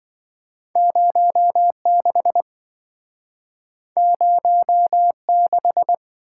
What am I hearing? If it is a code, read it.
Morse code, heard as a keyed tone, about 26 wpm, 700 Hz. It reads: 06 06